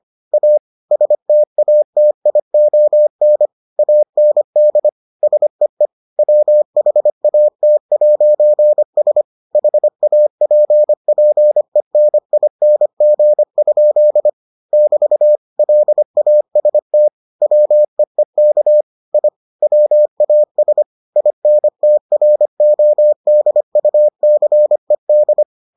Morse code, heard as a keyed tone, about 25 wpm, 600 Hz. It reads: A STATION AND SEE WHAT'S HAPPENING? <BT> LAST WEEK I WAS INTRODUCED